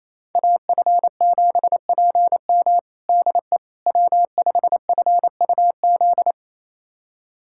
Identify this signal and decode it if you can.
Morse code, heard as a keyed tone, about 27 wpm, 700 Hz. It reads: AF7PM DE W5FUZ
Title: USA Callsign DE Callsign - 5 Characters - 28wpm